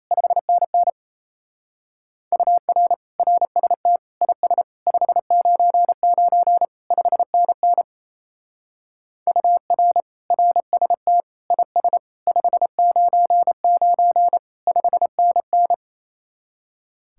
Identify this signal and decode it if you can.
Morse code, heard as a keyed tone, about 37 wpm, 700 Hz. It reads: 5NN UR RST IS 599 5NN UR RST IS 599 5NN